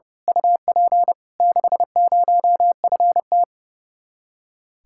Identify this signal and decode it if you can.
Morse code, heard as a keyed tone, about 31 wpm, 700 Hz. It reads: UP 60FT
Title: Speed-Racing Common QSO Phrases - 30wpm